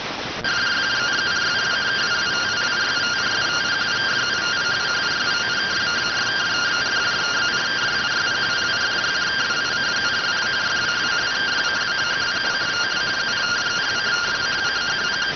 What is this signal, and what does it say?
Two-tone teleprinter signal, about 170 Hz shift.